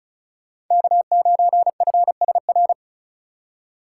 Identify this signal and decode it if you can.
Morse code, heard as a keyed tone, about 46 wpm, 700 Hz. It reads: K9FSR